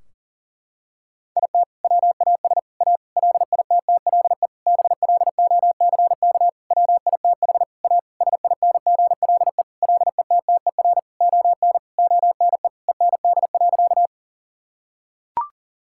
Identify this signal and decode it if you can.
Morse code, heard as a keyed tone, about 40 wpm, 700 Hz. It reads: IT WAS A LITTLE BLOCK WITH A SINGLE LETTER ON ONE END.
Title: Sentences from Top 1600 Words 40wpm